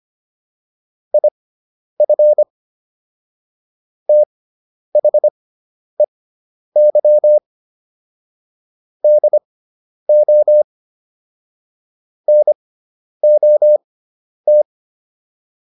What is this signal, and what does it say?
Morse code, heard as a keyed tone, about 25 wpm, spaced as for 10 wpm, 600 Hz. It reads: IF THEY DO NOT